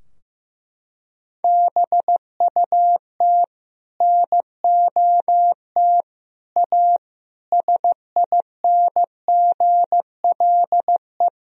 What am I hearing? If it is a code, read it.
Morse code, heard as a keyed tone, about 15 wpm, 700 Hz. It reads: BUT NOT A SINGLE
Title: Sentences from Top 1000 Words - Review Part 4 - 15wpm